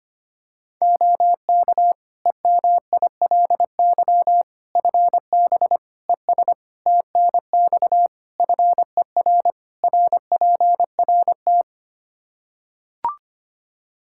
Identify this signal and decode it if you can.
Morse code, heard as a keyed tone, about 25 wpm, 700 Hz. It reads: OK EMILY FB ES TNX FER RPRT
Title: Practice QSO - Exchange 2 - Info - 25wpm